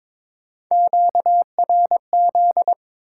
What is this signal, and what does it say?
Morse code, heard as a keyed tone, about 21 wpm, 700 Hz. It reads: QRZ
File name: Top 90 QSO Elements - 8x Character Spacing - 22wpm